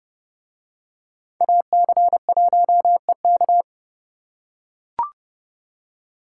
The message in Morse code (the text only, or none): AC1EK